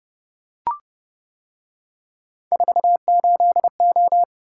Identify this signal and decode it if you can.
Morse code, heard as a keyed tone, about 29 wpm, 700 Hz. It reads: 48O